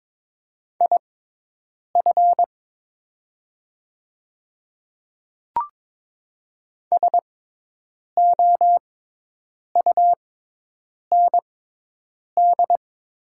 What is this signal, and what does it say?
Morse code, heard as a keyed tone, about 22 wpm, spaced as for 8 wpm, 700 Hz. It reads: IF SOUND
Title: 6x Character Spacing - Top 100 Words - 22wpm